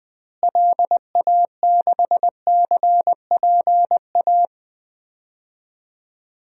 Morse code, heard as a keyed tone, about 21 wpm, 700 Hz. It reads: LA6CPA
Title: DXCC-100 Call Signs - 6 Characters - Repeated 2x - 20wpm